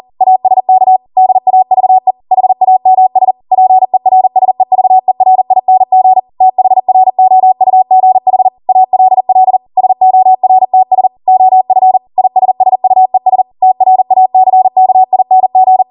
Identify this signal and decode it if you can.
Morse code, heard as a keyed tone, about 40 wpm, 750 Hz. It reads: ASK DAVE HAMS PERSEVERING THROUGH ALL SORTS OF ISSUES TRACKING
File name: Morse / 40WPM